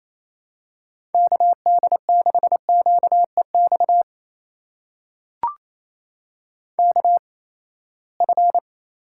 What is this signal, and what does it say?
Morse code, heard as a keyed tone, about 28 wpm, spaced as for 19 wpm, 700 Hz. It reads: KD6QEX KF
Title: USA Call Signs - 6 Characters - 8x Character Spacing - 28wpm